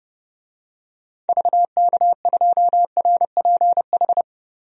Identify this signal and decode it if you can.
Morse code, heard as a keyed tone, about 32 wpm, 700 Hz. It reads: VK2RPH
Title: DXCC-100 Call Signs - 6 Characters - 30wpm